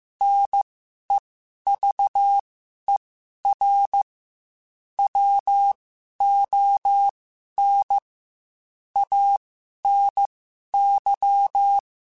Morse code, heard as a keyed tone, about 15 wpm, spaced as for 11 wpm, 800 Hz. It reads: NEVER WON ANY
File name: Quotes / Quotes-mp3s / Pessimism @13